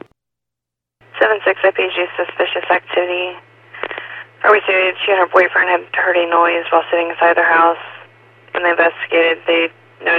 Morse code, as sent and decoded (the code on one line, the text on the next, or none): none